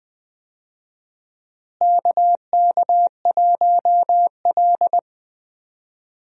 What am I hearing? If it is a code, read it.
Morse code, heard as a keyed tone, about 20 wpm, 700 Hz. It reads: KK1L
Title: USA Call Signs - 4 Characters - Repeated 2x - 20wpm